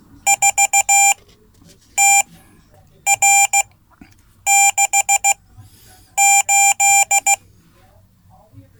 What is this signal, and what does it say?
Morse code, heard as a keyed tone, about 15 wpm, spaced as for 8 wpm, 790 Hz. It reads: 4TR68
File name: Code / 5min-c